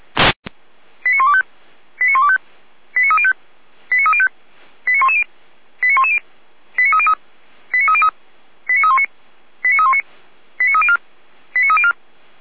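Five-tone selective call (ZVEI), heard as quick runs of five tones, tones of about 70 ms. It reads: 89125 89125 89305 89305 89110 89110 89303 89319 89319 89304 89304
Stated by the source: Five-tone ZVEI call 89276 received by SDR, recorded by Feuerwehr Orlamünde.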